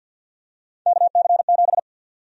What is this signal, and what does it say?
Morse code, heard as a keyed tone, about 52 wpm, 700 Hz. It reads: KC7
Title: USA Call Sign Prefixes - 50wpm